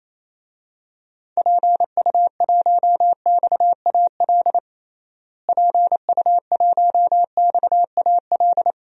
Morse code, heard as a keyed tone, about 28 wpm, 700 Hz. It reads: PU1XAL PU1XAL
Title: DXCC-100 Call Signs - 6 Characters - Repeated 3x - 28wpm